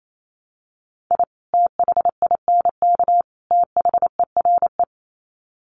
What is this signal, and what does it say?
Morse code, heard as a keyed tone, about 28 wpm, 700 Hz. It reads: I THINK THERE
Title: Sets of 3 Words 28wpm